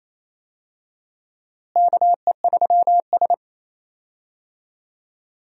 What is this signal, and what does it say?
Morse code, heard as a keyed tone, about 28 wpm, 700 Hz. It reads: KE3S
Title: USA Call Signs - 4 Characters - Repeated 2x - 28wpm